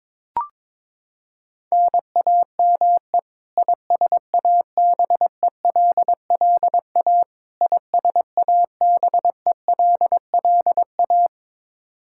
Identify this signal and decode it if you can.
Morse code, heard as a keyed tone, about 22 wpm, 700 Hz. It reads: NAME ISABELLA ISABELLA
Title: K1USN SST QSO Practice 22wpm